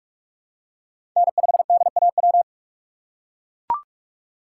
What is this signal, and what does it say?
Morse code, heard as a keyed tone, about 42 wpm, 700 Hz. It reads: N5DAW
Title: USA Call Signs - 5 Characters - 4x Character Spacing - 45wpm